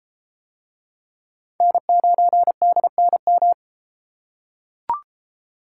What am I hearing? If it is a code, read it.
Morse code, heard as a keyed tone, about 32 wpm, 700 Hz. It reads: N9DNM